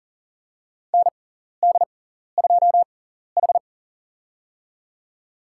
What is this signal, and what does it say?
Morse code, heard as a keyed tone, about 40 wpm, spaced as for 14 wpm, 700 Hz. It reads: ND2H
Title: USA Call Signs - 4 Characters - 6x Character Spacing - 40wpm